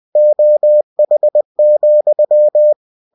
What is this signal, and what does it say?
Morse code, heard as a keyed tone, about 20 wpm, 600 Hz. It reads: OH,
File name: cw-books / winnie-the-pooh / winnie-20wpm-page-0065